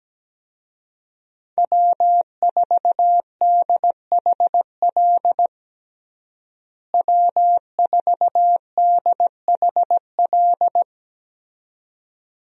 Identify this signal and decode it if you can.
Morse code, heard as a keyed tone, about 17 wpm, 700 Hz. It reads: W4DHL W4DHL
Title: USA Call Signs - 5 Characters - Repeated 2x - 17wpm